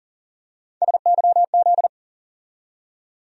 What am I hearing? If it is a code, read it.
Morse code, heard as a keyed tone, about 40 wpm, 700 Hz. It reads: SYZ
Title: USA Call Sign Postfixes - 3 Letters 40wpm